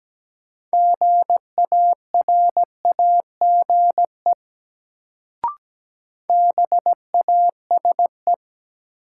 Morse code, heard as a keyed tone, about 17 wpm, 700 Hz. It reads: GARAGE BASE
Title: Sentences from Top 1600 Words - New Word Drill Alt - 17wpm